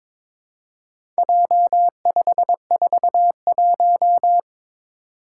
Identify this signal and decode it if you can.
Morse code, heard as a keyed tone, about 22 wpm, 700 Hz. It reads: J541
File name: practice / 22wpm / Four Letter-Number Combinations - 22wpm